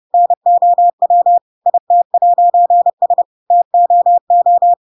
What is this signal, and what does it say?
Morse code, heard as a keyed tone, about 30 wpm, 700 Hz. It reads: NOW IT'S TOO
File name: Pinocchio / Pinocchio ch3 30 30 ik1hge